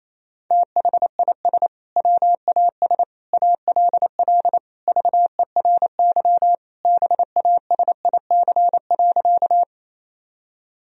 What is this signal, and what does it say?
Morse code, heard as a keyed tone, about 28 wpm, 700 Hz. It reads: THIS WAS ALL VERY BASIC.
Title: Sentences from Top 1000 Words - Review Part 3 - 28wpm